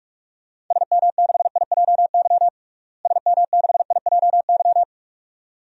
Morse code, heard as a keyed tone, about 47 wpm, 700 Hz. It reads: SM6IJY SM6IJY
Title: DXCC-100 Call Signs - 6 Characters - Repeated 2x - 45wpm